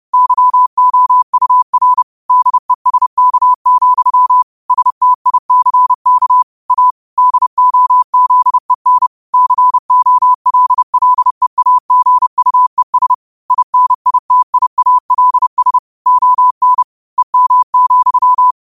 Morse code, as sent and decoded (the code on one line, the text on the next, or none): -.-- --- ..- .-. / -.. ... -.- --..-- / ... - .. -.-. -.- / .- / -.. --- --.. -. / -.-. --- .-.. .-.. .- --. ..- . ... / .. -. .. - .. .- .-.. ... / --- -. / . -- --..--
YOUR DESK, STICK A DOZEN COLLEAGUES INITIALS ON EM,